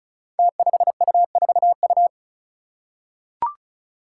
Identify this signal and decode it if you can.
Morse code, heard as a keyed tone, about 36 wpm, 700 Hz. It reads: T5U4U
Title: Five Challenging Letter-Number Combinations 35wpm